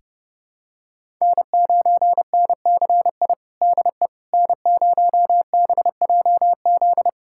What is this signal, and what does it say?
Morse code, heard as a keyed tone, about 32 wpm, 700 Hz. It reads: N9NCI DE N0BJZ
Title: USA Callsign DE Callsign - 5 Characters - 30wpm